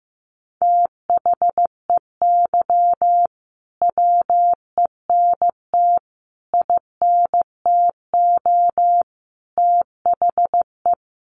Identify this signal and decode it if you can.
Morse code, heard as a keyed tone, about 15 wpm, 700 Hz. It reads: THEY WENT INTO THE
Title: Sentences from Top 900 Words 15wpm